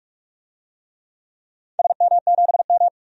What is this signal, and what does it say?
Morse code, heard as a keyed tone, about 46 wpm, 700 Hz. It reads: SM7M
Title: DXCC-100 Call Signs - 4 Characters - 45wpm